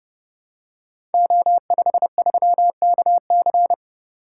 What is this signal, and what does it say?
Morse code, heard as a keyed tone, about 30 wpm, 700 Hz. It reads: O53KC
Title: Code Groups - 5 Letter:Numbers - 30wpm